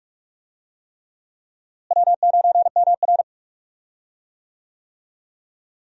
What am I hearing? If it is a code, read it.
Morse code, heard as a keyed tone, about 46 wpm, 700 Hz. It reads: W9MR
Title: USA Call Signs - 4 Characters - Repeated 3x - 45wpm